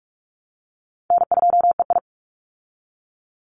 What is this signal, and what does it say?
Morse code, heard as a keyed tone, about 44 wpm, 700 Hz. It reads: N2EI
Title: Easy Call Signs Repeated 3x - 45wpm